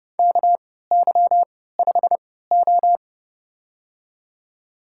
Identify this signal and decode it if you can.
Morse code, heard as a keyed tone, about 30 wpm, spaced as for 17 wpm, 700 Hz. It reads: KY5O
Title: USA Call Signs - 4 Characters - 3x Character Spacing - 30wpm